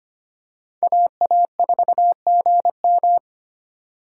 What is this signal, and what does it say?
Morse code, heard as a keyed tone, about 25 wpm, 700 Hz. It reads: AA4GM
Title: USA Call Signs - 4x Character Spacing - 25wpm